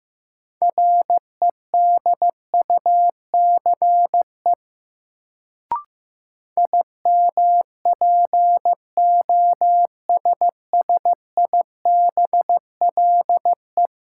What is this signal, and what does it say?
Morse code, heard as a keyed tone, about 15 wpm, 700 Hz. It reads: REDUCE IMPOSSIBLE
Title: Sentences from Top 1100 Words - New Word Drill Alt - 15wpm